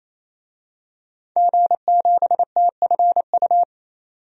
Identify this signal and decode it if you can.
Morse code, heard as a keyed tone, about 28 wpm, 700 Hz. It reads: G7TFU
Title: DXCC-100 Call Signs - 5 Characters - 28wpm